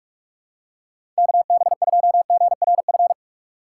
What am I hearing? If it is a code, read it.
Morse code, heard as a keyed tone, about 46 wpm, 700 Hz. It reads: KB2GRF
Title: USA Call Signs - 6 Characters - 2x Character Spacing - 45wpm